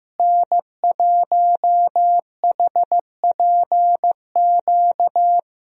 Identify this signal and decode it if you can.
Morse code, heard as a keyed tone, about 15 wpm, 700 Hz. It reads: N1HPQ
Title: USA Call Signs - 5 Characters - Repeated 2x - 15wpm